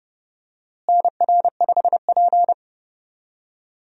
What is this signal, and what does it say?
Morse code, heard as a keyed tone, about 30 wpm, 700 Hz. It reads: NR5P
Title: USA Call Signs - 4 Characters - Repeated 3x - 30wpm